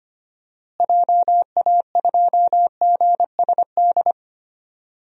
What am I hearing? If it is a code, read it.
Morse code, heard as a keyed tone, about 25 wpm, 700 Hz. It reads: JA2GSD
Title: DXCC-100 Call Signs - 6 Characters - Repeated 2x - 25wpm